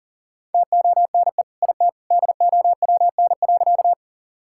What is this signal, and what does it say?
Morse code, heard as a keyed tone, about 40 wpm, 700 Hz. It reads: TONE IT DOWN.
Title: Sentences from Top 1600 Words 40wpm